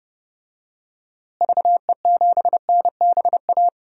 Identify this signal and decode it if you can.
Morse code, heard as a keyed tone, about 32 wpm, 700 Hz. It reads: VE7NBA
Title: DXCC-100 Call Signs - 6 Characters - Repeated 2x - 30wpm